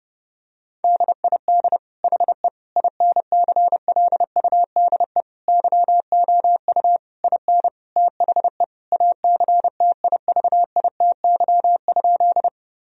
Morse code, heard as a keyed tone, about 30 wpm, 700 Hz. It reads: DID HE INCLUDE YOU IN THE ACTIVITY?
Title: Sentences from Top 1000 Words - Review Part 4 - 30wpm